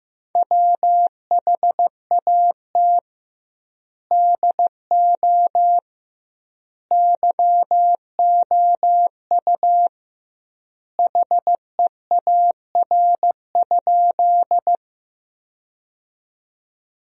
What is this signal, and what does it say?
Morse code, heard as a keyed tone, about 15 wpm, 700 Hz. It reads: WHAT DO YOU HEAR?
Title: 2x Word-Spacing - Sentences from Top 300 Words - 15wpm